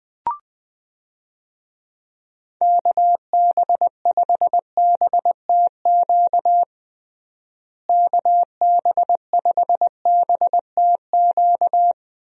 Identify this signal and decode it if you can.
Morse code, heard as a keyed tone, about 21 wpm, 700 Hz. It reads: KB5BTQ KB5BTQ
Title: USA Call Signs - 6 Characters - Repeated 3x - 20wpm